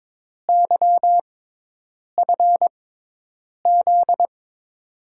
Morse code, heard as a keyed tone, about 22 wpm, spaced as for 8 wpm, 700 Hz. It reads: YFZ
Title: USA Call Signs - 5 Characters - 6x Character Spacing - 22wpm